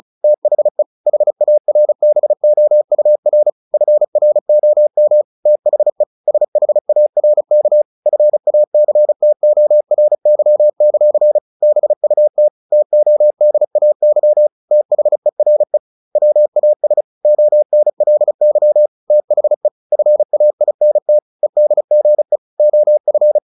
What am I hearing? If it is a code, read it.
Morse code, heard as a keyed tone, about 35 wpm, 600 Hz. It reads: THE HARBOUR FROM THE SHARK FACTORY; BUT TODAY THERE WAS ONLY THE FAINT EDGE OF
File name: the old man and the sea / the old man and the sea 1 35wpm